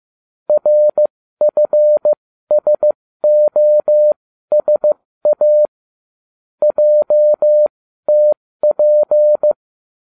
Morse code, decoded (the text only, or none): RFSOSA JTP